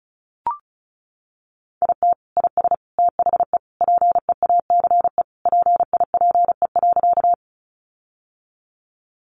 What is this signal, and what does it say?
Morse code, heard as a keyed tone, about 34 wpm, 700 Hz. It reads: IT IS THE PEACE PIPE.
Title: Sentences from Top 1500 Words 35wpm